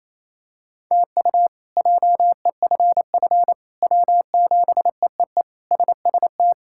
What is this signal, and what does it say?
Morse code, heard as a keyed tone, about 28 wpm, 700 Hz. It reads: TU JEFF W7EEE SST
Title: K1USN SST QSO Practice - 28wpm